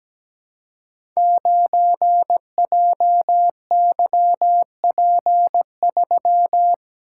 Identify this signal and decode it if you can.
Morse code, heard as a keyed tone, about 17 wpm, 700 Hz. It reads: 9JYP3
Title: Code Groups - 5 Letter:Numbers - 17wpm